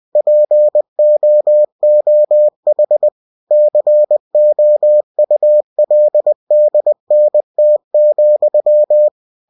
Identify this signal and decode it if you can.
Morse code, heard as a keyed tone, about 20 wpm, 600 Hz. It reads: POOH COULDNT,
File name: cw-books / winnie-the-pooh / winnie-20wpm-page-0011